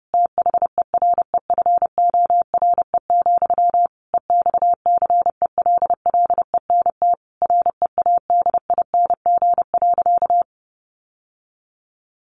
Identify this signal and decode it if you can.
Morse code, heard as a keyed tone, about 30 wpm, 700 Hz. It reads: THEREFORE, EXCELLENT READING.
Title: Sentences from Top 1000 Words 30wpm